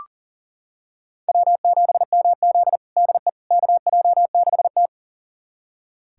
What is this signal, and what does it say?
Morse code, heard as a keyed tone, about 40 wpm, 700 Hz. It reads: W7MZ DE KJ6T
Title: USA Callsign DE Callsign - 4 Characters - 40wpm